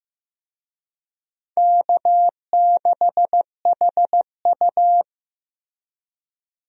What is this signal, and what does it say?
Morse code, heard as a keyed tone, about 15 wpm, 700 Hz. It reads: K6HU
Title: USA Call Signs - 4 Characters - Repeated 3x - 15wpm